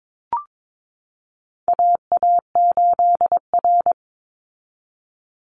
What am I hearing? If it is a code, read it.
Morse code, heard as a keyed tone, about 22 wpm, 700 Hz. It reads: AA8R
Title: Call Signs Repeated 3x - 22wpm